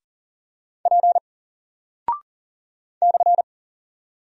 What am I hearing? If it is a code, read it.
Morse code, heard as a keyed tone, about 40 wpm, 700 Hz. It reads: P /
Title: Single Character (Letter:Number:Punctuation:Prosign) Alt 40wpm